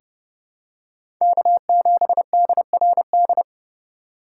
Morse code, heard as a keyed tone, about 30 wpm, 700 Hz. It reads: K7DRD
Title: USA Call Signs - 5 Characters - 3x Character Spacing - 30wpm